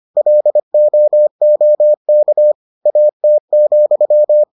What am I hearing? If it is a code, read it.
Morse code, heard as a keyed tone, about 25 wpm, 600 Hz. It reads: LOOK AT,